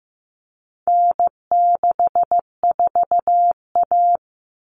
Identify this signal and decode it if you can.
Morse code, heard as a keyed tone, about 15 wpm, 700 Hz. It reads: N64A
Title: Common Cut Numbers - 4-Digit 15wpm